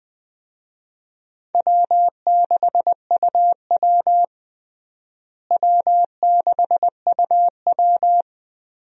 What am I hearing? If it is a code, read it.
Morse code, heard as a keyed tone, about 20 wpm, 700 Hz. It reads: W6UW W6UW
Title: USA Call Signs - 4 Characters - Repeated 3x - 20wpm